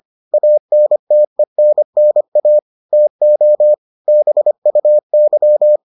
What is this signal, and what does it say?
Morse code, heard as a keyed tone, about 25 wpm, 600 Hz. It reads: ANTENNA TO BUY